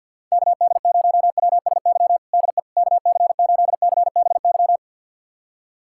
Morse code, heard as a keyed tone, about 46 wpm, 700 Hz. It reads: KD0WSY DE KC8CBY
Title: USA Callsign DE Callsign 50wpm